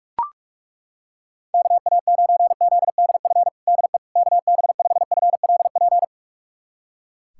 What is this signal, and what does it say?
Morse code, heard as a keyed tone, about 42 wpm, 700 Hz. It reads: KA9ZDF DE KB5FLP
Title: USA Callsign DE Callsign - 6 Characters - 45wpm